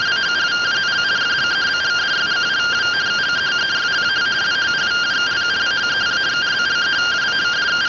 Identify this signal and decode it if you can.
Two-tone teleprinter signal, about 170 Hz shift.